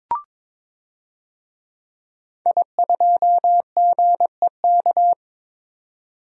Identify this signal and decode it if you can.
Morse code, heard as a keyed tone, about 22 wpm, 700 Hz. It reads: I2GEK
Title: DXCC-100 Call Signs - 5 Characters - 22wpm